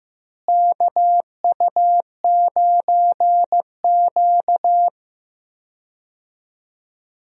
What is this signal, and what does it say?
Morse code, heard as a keyed tone, about 15 wpm, 700 Hz. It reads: KU9Q
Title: USA Call Signs - 4 Characters - Repeated 2x - 15wpm